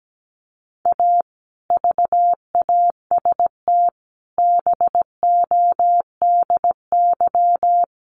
Morse code, heard as a keyed tone, about 17 wpm, 700 Hz. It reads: A VAST BODY
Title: Sentences from Top 1300 Words 17wpm